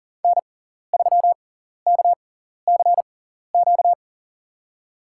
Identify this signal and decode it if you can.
Morse code, heard as a keyed tone, about 38 wpm, spaced as for 14 wpm, 700 Hz. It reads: N3KCQ
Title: USA Call Signs - 5 Characters - 6x Character Spacing - 40wpm